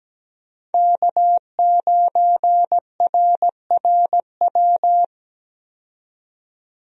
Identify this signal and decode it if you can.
Morse code, heard as a keyed tone, about 17 wpm, 700 Hz. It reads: K9RRW
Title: USA Call Signs - 5 Characters - 17wpm